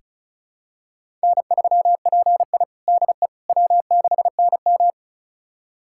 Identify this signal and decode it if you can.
Morse code, heard as a keyed tone, about 36 wpm, 700 Hz. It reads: N3PI DE W6NM
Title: USA Callsign DE Callsign - 4 Characters - 35wpm